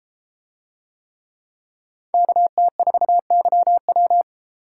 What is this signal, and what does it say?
Morse code, heard as a keyed tone, about 30 wpm, 700 Hz. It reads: KT4YW